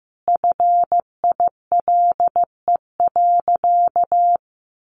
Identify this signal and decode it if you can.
Morse code, heard as a keyed tone, about 15 wpm, 700 Hz. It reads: FILE.
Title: Sentences from Top 1200 Words 15wpm